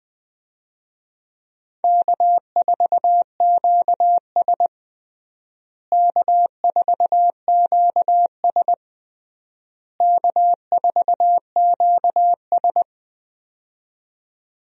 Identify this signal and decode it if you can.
Morse code, heard as a keyed tone, about 21 wpm, 700 Hz. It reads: K4QS K4QS K4QS
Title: USA Call Signs - 4 Characters - Repeated 3x - 20wpm